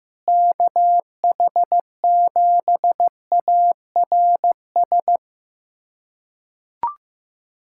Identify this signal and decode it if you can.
Morse code, heard as a keyed tone, about 15 wpm, 700 Hz. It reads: KH7ARS
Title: USA Callsign DE Callsign - 6 Characters - 15wpm